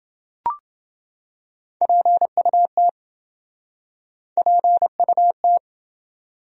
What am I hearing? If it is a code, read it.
Morse code, heard as a keyed tone, about 29 wpm, 700 Hz. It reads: PUT PUT